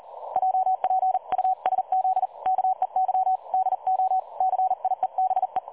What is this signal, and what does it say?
Morse code, heard as a keyed tone, about 41 wpm, 750 Hz. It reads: JPUSZ KEY DO CIEBE
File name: dlazaawansowanych / nagraniazpasma / websdr recording 3565.0kHz